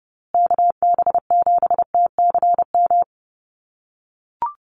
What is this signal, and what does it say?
Morse code, heard as a keyed tone, about 30 wpm, 700 Hz. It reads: KB7TCM
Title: USA Call Signs Repeated 2x 30wpm